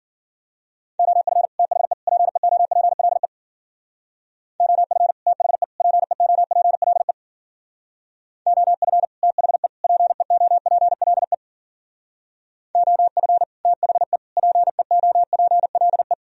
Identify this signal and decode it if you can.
Morse code, heard as a keyed tone, about 48 wpm, 700 Hz. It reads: OF THE PEOPLE OF THE PEOPLE OF THE PEOPLE OF THE PEOPLE
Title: Speed-Racing - Sets of 3 Words 40wpm